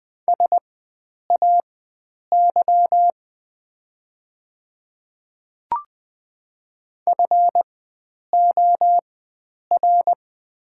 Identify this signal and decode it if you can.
Morse code, heard as a keyed tone, about 20 wpm, spaced as for 9 wpm, 700 Hz. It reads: SAY FOR
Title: 4x Character Spacing - Top 300 Words - 20wpm